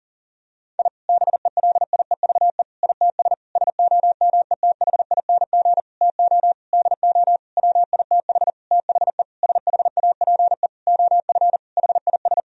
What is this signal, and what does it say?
Morse code, heard as a keyed tone, about 40 wpm, 700 Hz. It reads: I BELIEVE ITS SOMETHING TO DO WITH THE SHAPE OF HIS
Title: Sentences from Top 1000 Words - Review Part 3 - 40wpm